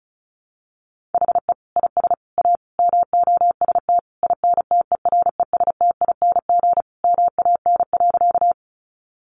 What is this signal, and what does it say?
Morse code, heard as a keyed tone, about 34 wpm, 700 Hz. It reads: HE IS A MOST INTERESTING MAN.
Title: Sentences from Top 1000 Words 35wpm